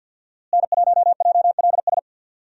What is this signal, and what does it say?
Morse code, heard as a keyed tone, about 48 wpm, 700 Hz. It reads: N1JLS
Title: USA Call Signs - 5 Characters - Repeated 2x - 50wpm